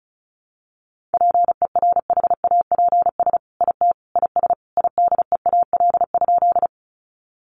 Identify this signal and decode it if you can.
Morse code, heard as a keyed tone, about 34 wpm, 700 Hz. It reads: PERHAPS IT IS IDEAL?
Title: Sentences from Top 1400 Words 35wpm